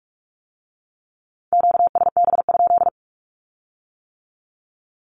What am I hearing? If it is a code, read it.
Morse code, heard as a keyed tone, about 46 wpm, 700 Hz. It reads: QSB?
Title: Q-Codes with Spoken Meaning - 45wpm